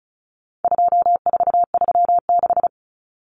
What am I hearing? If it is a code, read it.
Morse code, heard as a keyed tone, about 34 wpm, 700 Hz. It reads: 2436
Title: Numbers - 4-Digit 35wpm